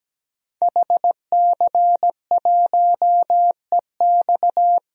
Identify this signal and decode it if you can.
Morse code, heard as a keyed tone, about 17 wpm, 700 Hz. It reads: HC1EX
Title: DXCC-100 Call Signs - 5 Characters - 17wpm